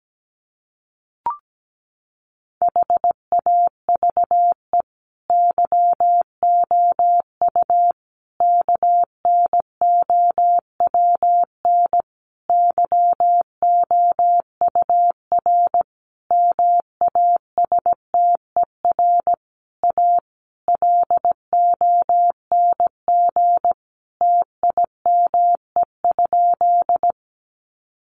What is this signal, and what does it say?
Morse code, heard as a keyed tone, about 17 wpm, 700 Hz. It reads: HAVE YOU KNOWN YOUR MASTER A LONG TIME?
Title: Sentences from Top 1300 Words 17wpm